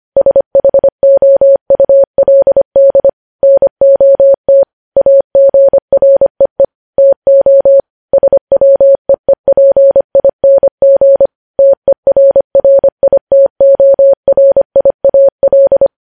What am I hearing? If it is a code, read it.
Morse code, heard as a keyed tone, about 25 wpm, 550 Hz. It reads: SHOULD NOT AGREE TO SWEEPING TERRITORIAL
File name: podcast / News headlines-1x-25x25